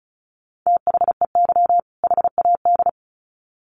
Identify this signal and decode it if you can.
Morse code, heard as a keyed tone, about 34 wpm, 700 Hz. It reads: THEY HAD
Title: Sets of 2 Words 35wpm